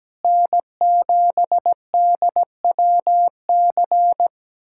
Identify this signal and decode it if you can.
Morse code, heard as a keyed tone, about 17 wpm, 700 Hz. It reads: N7DWC
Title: USA Callsign DE Callsign 17wpm